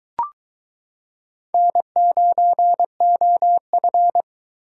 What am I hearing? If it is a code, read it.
Morse code, heard as a keyed tone, about 22 wpm, 700 Hz. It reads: N9OF